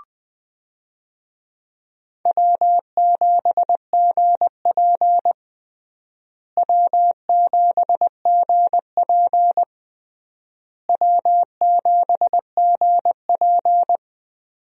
Morse code, heard as a keyed tone, about 20 wpm, 700 Hz. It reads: W7GP W7GP W7GP
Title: USA Call Signs - 4 Characters - Repeated 3x - 20wpm